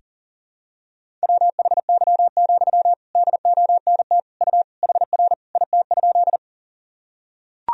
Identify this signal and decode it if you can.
Morse code, heard as a keyed tone, about 38 wpm, 700 Hz. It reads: WHY, DONT U HR IT?
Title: Sentences from Top 300 Words and Abbreviations 40wpm